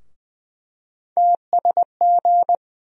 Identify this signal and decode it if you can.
Morse code, heard as a keyed tone, about 20 wpm, 700 Hz. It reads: TSG
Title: USA Call Sign Postfixes - 3 Letters 20wpm